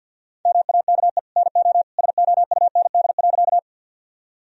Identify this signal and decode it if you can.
Morse code, heard as a keyed tone, about 50 wpm, 700 Hz. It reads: MAKE NO SOUND.
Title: Sentences from Top 1000 Words - Review Part 1 - 50wpm